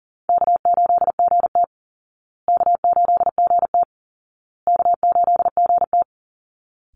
Morse code, heard as a keyed tone, about 40 wpm, 700 Hz. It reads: K8GT K8GT K8GT
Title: Easy Call Signs Repeated 3x - 40wpm